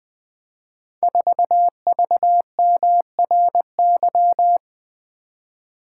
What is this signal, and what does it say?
Morse code, heard as a keyed tone, about 20 wpm, 700 Hz. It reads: 4VMRY
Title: Code Groups - 5 Letter:Numbers - 20wpm